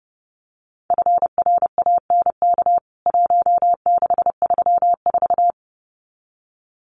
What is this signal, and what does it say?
Morse code, heard as a keyed tone, about 30 wpm, 700 Hz. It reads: FRANK 1634
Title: CWT Exchange 30wpm